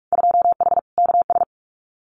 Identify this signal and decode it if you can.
Morse code, heard as a keyed tone, about 46 wpm, 690 Hz. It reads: 2H KS
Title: Winter Field Day Exchanges Encore 45wpm